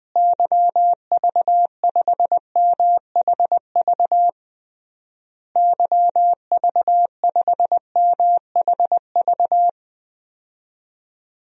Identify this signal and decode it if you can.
Morse code, heard as a keyed tone, about 20 wpm, 700 Hz. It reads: YV5MHV YV5MHV
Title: DXCC-100 Call Signs - 6 Characters - Repeated 3x - 20wpm